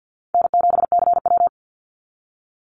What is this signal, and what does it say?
Morse code, heard as a keyed tone, about 50 wpm, 700 Hz. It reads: N7CW
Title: NA SOTA Chaser Callsigns 50wpm